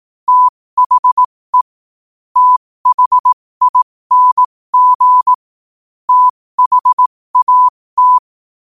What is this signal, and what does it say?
Morse code, heard as a keyed tone, about 18 wpm, spaced as for 15 wpm, 1000 Hz. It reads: THE THING THAT
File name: qotdfiles / 15wpm